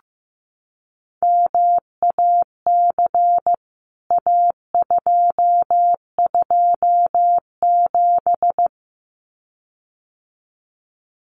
Morse code, heard as a keyed tone, about 15 wpm, 700 Hz. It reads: MAC A227
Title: CWT Exchange 15wpm